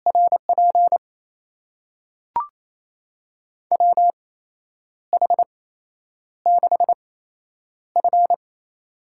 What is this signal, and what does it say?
Morse code, heard as a keyed tone, about 28 wpm, spaced as for 9 wpm, 700 Hz. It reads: RP WH6F
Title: USA Call Signs - 8x Character Spacing - 28wpm